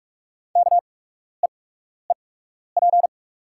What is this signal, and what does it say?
Morse code, heard as a keyed tone, about 48 wpm, spaced as for 12 wpm, 700 Hz. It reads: KEEP
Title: 8x Character Spacing - Top 300 Words - 45wpm